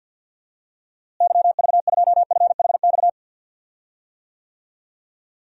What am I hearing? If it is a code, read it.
Morse code, heard as a keyed tone, about 53 wpm, 700 Hz. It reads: YV2FHX
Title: DXCC-100 Call Signs - 6 Characters - Repeated 2x - 50wpm